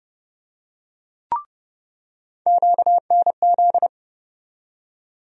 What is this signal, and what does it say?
Morse code, heard as a keyed tone, about 31 wpm, 700 Hz. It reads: QNZ
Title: USA Call Sign Postfixes - 3 Letters 30wpm